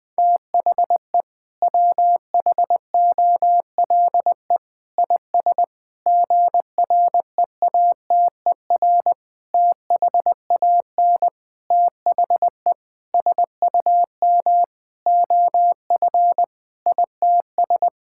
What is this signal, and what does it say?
Morse code, heard as a keyed tone, about 20 wpm, 700 Hz. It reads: THE WHOLE IS GREATER THAN THE SUM OF ITS
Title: Sentences with 6-Word Phrases - 20wpm